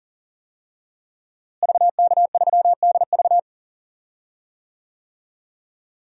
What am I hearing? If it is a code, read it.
Morse code, heard as a keyed tone, about 40 wpm, 700 Hz. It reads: VK3DV
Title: DXCC-100 Call Signs - 5 Characters - Repeated 2x - 40wpm